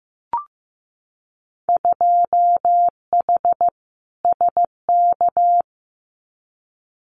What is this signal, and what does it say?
Morse code, heard as a keyed tone, about 14 wpm, 700 Hz. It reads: 2H SK
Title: Winter Field Day Exchanges Encore 15wpm